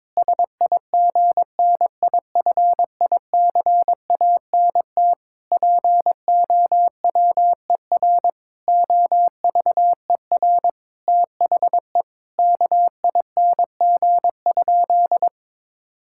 Morse code, heard as a keyed tone, about 22 wpm, 700 Hz. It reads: SIGNIFICANT POWER OVER THE KING?
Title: Sentences from Top 1000 Words - Review Part 6 - 22wpm